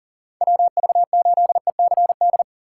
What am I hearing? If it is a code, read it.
Morse code, heard as a keyed tone, about 40 wpm, 700 Hz. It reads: WV8ECD
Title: USA Call Signs - 6 Characters - 3x Character Spacing - 40wpm